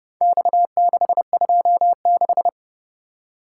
Morse code, heard as a keyed tone, about 28 wpm, 700 Hz. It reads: X626